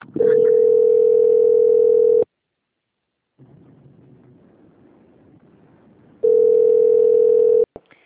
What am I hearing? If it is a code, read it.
Ringback tone (North America), 440 + 480 Hz.